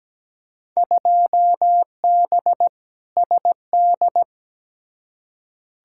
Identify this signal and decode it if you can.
Morse code, heard as a keyed tone, about 17 wpm, 700 Hz. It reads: 2B SD
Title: Field Day Exchanges Encore - 17wpm